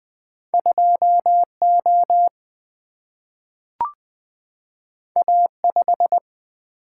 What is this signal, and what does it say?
Morse code, heard as a keyed tone, about 20 wpm, 700 Hz. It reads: 2O A5
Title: Single-Letter Single-Number 20wpm